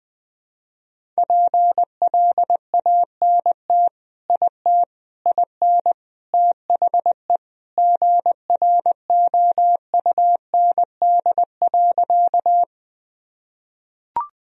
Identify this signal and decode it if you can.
Morse code, heard as a keyed tone, about 21 wpm, 700 Hz. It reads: PLANT IT IN THE GROUND.
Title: Sentences from Top 1000 Words - Review Part 1 - 20wpm